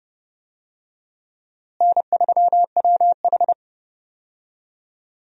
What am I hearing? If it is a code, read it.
Morse code, heard as a keyed tone, about 30 wpm, 700 Hz. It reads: N3WH
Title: USA Call Signs - 4 Characters - Repeated 3x - 30wpm